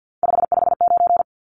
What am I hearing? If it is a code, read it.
Morse code, heard as a keyed tone, about 50 wpm, 700 Hz. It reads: 559
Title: Numbers - 3-Digit 50wpm